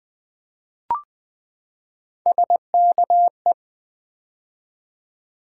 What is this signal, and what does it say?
Morse code, heard as a keyed tone, about 20 wpm, 700 Hz. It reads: SKE
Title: Build A Word - 3 Letters At A Time - 6-Letter Words - 20wpm